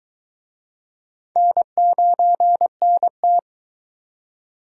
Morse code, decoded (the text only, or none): N9NT